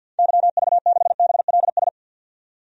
Morse code, heard as a keyed tone, about 52 wpm, 700 Hz. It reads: YV6BLS
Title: DXCC-100 Call Signs - 6 Characters - 50wpm